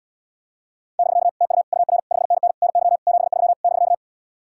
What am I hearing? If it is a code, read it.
Morse code, heard as a keyed tone, about 80 wpm, 700 Hz. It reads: <BK> TU UR 5NN NY 73 <BK>